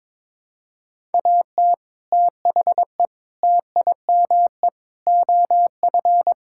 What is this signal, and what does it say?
Morse code, heard as a keyed tone, about 22 wpm, 700 Hz. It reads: AT THE TIME OF
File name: practice / 22wpm / Sets of 6 Words - 22wpm